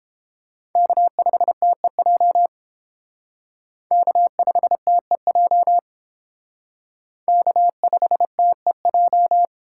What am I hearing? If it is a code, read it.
Morse code, heard as a keyed tone, about 29 wpm, 700 Hz. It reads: K5TEJ K5TEJ K5TEJ